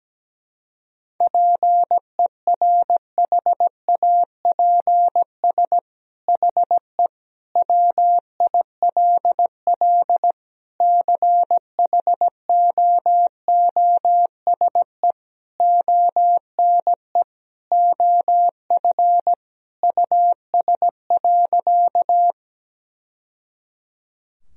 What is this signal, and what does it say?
Morse code, heard as a keyed tone, about 17 wpm, 700 Hz. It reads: PERHAPS HE WILL CHOOSE ONE OF US.
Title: Sentences from Top 1000 Words - Review Part 4 - 17wpm